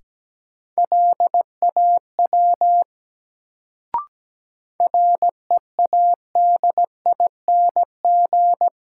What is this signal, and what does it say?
Morse code, heard as a keyed tone, about 17 wpm, 700 Hz. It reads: LAW READING
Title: Sentences from Top 600 Words - New Word Drill - 17wpm